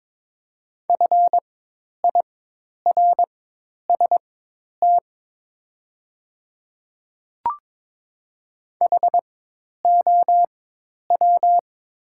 Morse code, heard as a keyed tone, about 22 wpm, spaced as for 10 wpm, 700 Hz. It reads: FIRST HOW
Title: 4x Character Spacing - Top 100 Words - 22wpm